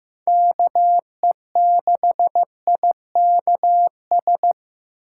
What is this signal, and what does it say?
Morse code, heard as a keyed tone, about 18 wpm, 700 Hz. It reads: KE6IKS
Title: Speed Racing - USA 2x3 Call Signs 15wpm